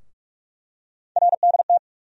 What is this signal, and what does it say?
Morse code, heard as a keyed tone, about 42 wpm, 700 Hz. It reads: RDT
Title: USA Call Sign Postfixes - 3 Letters 45wpm